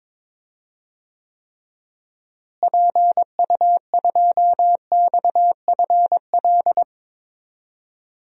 Morse code, heard as a keyed tone, about 22 wpm, 700 Hz. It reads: PU2XFL
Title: DXCC-100 Call Signs - 6 Characters - Repeated 2x - 22wpm